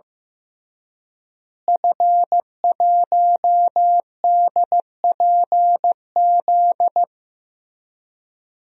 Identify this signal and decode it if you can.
Morse code, heard as a keyed tone, about 15 wpm, 700 Hz. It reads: F1DPZ
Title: DXCC-100 Call Signs - 5 Characters - Repeated 3x - 15wpm